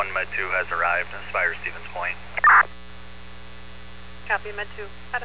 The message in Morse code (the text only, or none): none